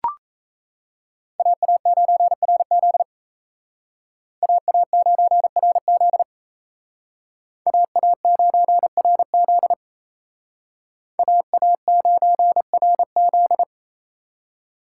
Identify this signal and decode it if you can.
Morse code, heard as a keyed tone, about 35 wpm, 700 Hz. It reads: AA9RZ AA9RZ AA9RZ AA9RZ